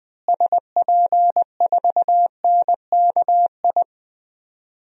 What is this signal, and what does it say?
Morse code, heard as a keyed tone, about 20 wpm, 700 Hz. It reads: SP4NKI
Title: DXCC-100 Call Signs - 6 Characters - Repeated 2x - 20wpm